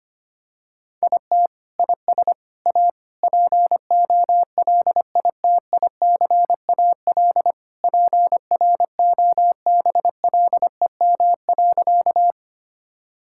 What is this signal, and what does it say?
Morse code, heard as a keyed tone, about 25 wpm, 700 Hz. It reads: IT IS A POLITICAL PROBLEM.
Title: Sentences from Top 1000 Words - Review Part 5 - 25wpm